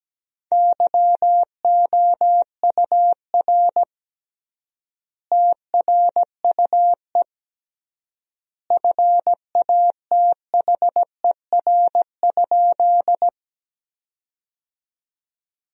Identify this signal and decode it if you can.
Morse code, heard as a keyed tone, about 17 wpm, 700 Hz. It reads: YOUR TRUE FATHER?
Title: 3x Word-Spacing - Sentences from Top 400 Words - 17wpm